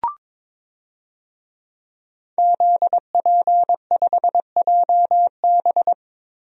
Morse code, heard as a keyed tone, about 22 wpm, 700 Hz. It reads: ZP5JB
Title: DXCC-100 Call Signs - 5 Characters - Repeated 2x - 22wpm